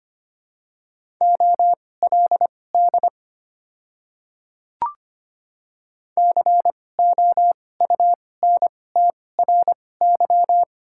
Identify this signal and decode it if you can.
Morse code, heard as a keyed tone, about 25 wpm, spaced as for 18 wpm, 700 Hz. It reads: OLD COUNTRY
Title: 2x Character Spacing - Top 300 Words - 25wpm